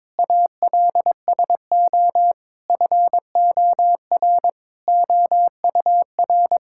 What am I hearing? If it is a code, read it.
Morse code, heard as a keyed tone, about 22 wpm, 700 Hz. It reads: ALSO FOR OUR
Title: Sentences from Top 1000 Words - Review Part 6 - 22wpm